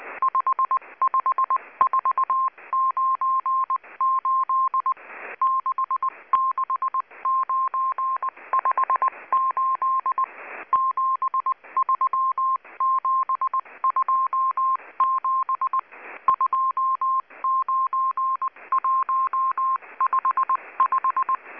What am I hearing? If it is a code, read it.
Morse code, heard as a keyed tone, about 19 wpm, 1060 Hz. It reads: 55498 66958 73727 29155
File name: HM02- -0540utc